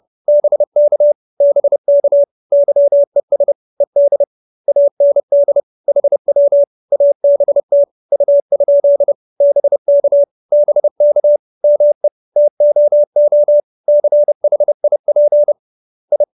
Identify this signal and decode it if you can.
Morse code, heard as a keyed tone, about 30 wpm, 590 Hz. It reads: BK BK YES ED AND HW ABT U? BK BK ME TOO CHIP I
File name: QSO316 30